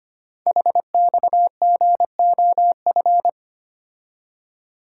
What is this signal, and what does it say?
Morse code, heard as a keyed tone, about 25 wpm, 700 Hz. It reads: HXGOF
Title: Code Groups - 5 Letters - 25wpm